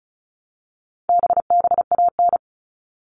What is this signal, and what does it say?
Morse code, heard as a keyed tone, about 34 wpm, 700 Hz. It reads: BBAN
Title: All Cut Numbers - 4-Digit 35wpm